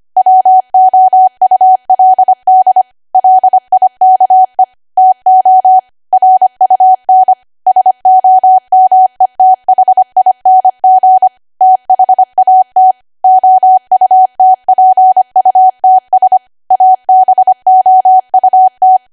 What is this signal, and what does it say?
Morse code, heard as a keyed tone, about 25 wpm, 750 Hz. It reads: WOULD LIKE TO RUN SOMETHING THAT OUTPUTS ABOUT
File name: Archive / 25 WPM / 25WPM